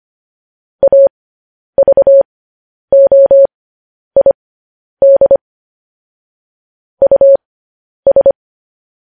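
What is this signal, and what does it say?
Morse code, heard as a keyed tone, about 25 wpm, spaced as for 10 wpm, 550 Hz. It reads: AVOID US
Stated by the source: Morse code practice of news headlines at 10 wpm.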